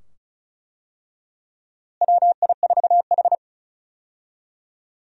Morse code, heard as a keyed tone, about 34 wpm, 700 Hz. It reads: WI4H
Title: USA Call Signs - 4 Characters - Repeated 2x - 35wpm